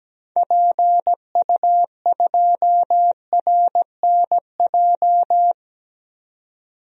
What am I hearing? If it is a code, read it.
Morse code, heard as a keyed tone, about 17 wpm, 700 Hz. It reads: PU2RNJ
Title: DXCC-100 Call Signs - 6 Characters - 17wpm